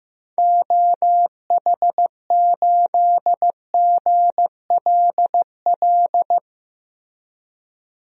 Morse code, heard as a keyed tone, about 15 wpm, 700 Hz. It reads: OH8GLL
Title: DXCC-100 Call Signs - 6 Characters - Repeated 3x - 15wpm